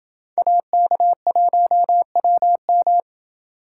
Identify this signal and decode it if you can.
Morse code, heard as a keyed tone, about 28 wpm, 700 Hz. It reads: AK1WM